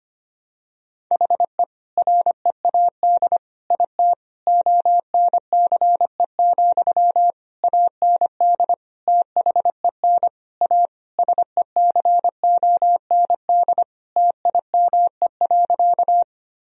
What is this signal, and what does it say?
Morse code, heard as a keyed tone, about 25 wpm, 700 Hz. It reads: HE READ IT ONCE, AND THEN A SECOND TIME.
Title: Sentences from Top 1000 Words - Review Part 1 - 25wpm